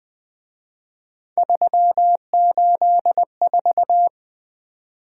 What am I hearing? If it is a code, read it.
Morse code, heard as a keyed tone, about 23 wpm, 700 Hz. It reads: 384
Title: Speed-Racing - Three Letter-Number Combinations - 20wpm